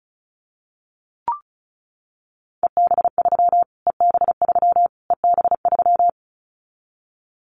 Morse code, heard as a keyed tone, about 34 wpm, 700 Hz. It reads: EB3 EB3 EB3
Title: Callsign Prefixes for Top 25 DXCC - Repeated 3x - 35wpm